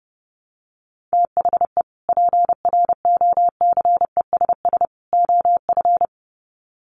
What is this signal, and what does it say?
Morse code, heard as a keyed tone, about 30 wpm, 700 Hz. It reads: THE PROCESS OF
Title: Sets of 3 Words 30wpm